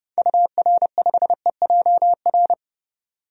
Morse code, heard as a keyed tone, about 31 wpm, 700 Hz. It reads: UR5EJR
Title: DXCC-100 Call Signs - 6 Characters - 30wpm